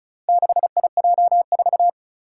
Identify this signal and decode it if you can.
Morse code, heard as a keyed tone, about 36 wpm, 700 Hz. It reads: 6IJ4
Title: Four Challenging Letter-Number Combinations 35wpm